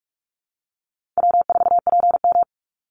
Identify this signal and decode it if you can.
Morse code, heard as a keyed tone, about 44 wpm, 700 Hz. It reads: W4PM
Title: Easy Call Signs - 45wpm